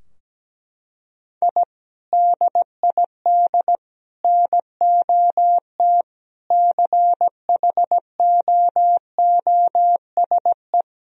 Morse code, heard as a keyed tone, about 17 wpm, 700 Hz. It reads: I DID NOT CHOOSE
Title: Sentences from Top 1000 Words - Review Part 4 - 17wpm